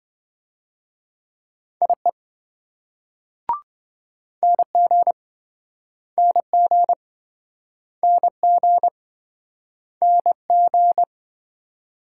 Morse code, heard as a keyed tone, about 26 wpm, 700 Hz. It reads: IE NG NG NG NG